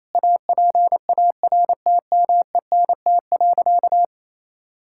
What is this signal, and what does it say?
Morse code, heard as a keyed tone, about 27 wpm, 700 Hz. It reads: APARTMENT.
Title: Sentences from Top 1000 Words - Review Part 6 - 28wpm